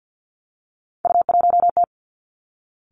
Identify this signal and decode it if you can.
Morse code, heard as a keyed tone, about 50 wpm, 700 Hz. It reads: U1T
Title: Three Challenging Letter-Number Combinations 50wpm